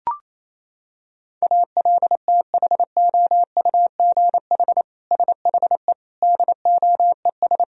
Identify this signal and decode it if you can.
Morse code, heard as a keyed tone, about 27 wpm, 700 Hz. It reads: ALTHOUGH SHE DOES
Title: Sentences from Top 1000 Words - Review Part 3 - 28wpm